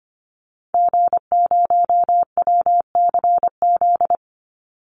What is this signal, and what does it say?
Morse code, heard as a keyed tone, about 25 wpm, 700 Hz. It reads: G0WCZ
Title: Call Signs - 25wpm